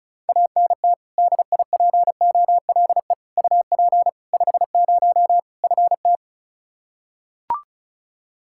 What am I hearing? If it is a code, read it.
Morse code, heard as a keyed tone, about 36 wpm, 700 Hz. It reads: ANT DIPOLE UP 50 FT
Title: Practice QSO - Exchange 2 - Info - 35wpm